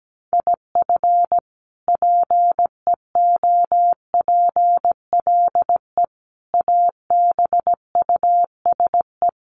Morse code, heard as a keyed tone, about 17 wpm, 700 Hz. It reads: IF PEOPLE ABUSE
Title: Sentences from Top 1300 Words 17wpm